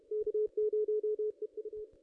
Morse code, heard as a keyed tone, about 33 wpm, 430 Hz, with radio noise. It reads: K0EU
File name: ss-cw-k5tr-op-n5rz